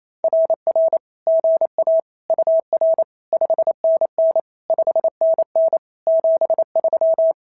Morse code, heard as a keyed tone, about 28 wpm, 650 Hz. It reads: RR GA UR 5NN 5NN 73
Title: SOTA QSO Practice 28wpm